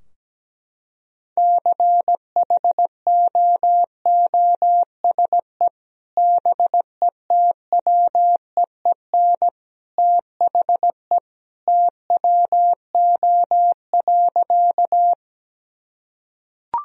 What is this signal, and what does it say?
Morse code, heard as a keyed tone, about 17 wpm, 700 Hz. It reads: CHOOSE BETWEEN THE TWO.
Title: Sentences from Top 1000 Words - Review Part 4 - 17wpm